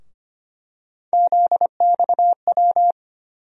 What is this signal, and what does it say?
Morse code, heard as a keyed tone, about 25 wpm, 700 Hz. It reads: ZXW
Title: USA Call Sign Postfixes - 3 Letters 25wpm